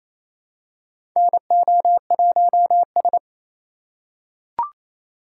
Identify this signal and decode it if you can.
Morse code, heard as a keyed tone, about 28 wpm, 700 Hz. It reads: NO1S
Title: USA Call Signs - 4 Characters - 4x Character Spacing - 28wpm